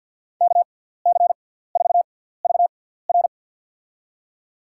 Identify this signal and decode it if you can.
Morse code, heard as a keyed tone, about 53 wpm, spaced as for 17 wpm, 700 Hz. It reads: KC4VR
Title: USA Call Signs - 6x Character Spacing - 50wpm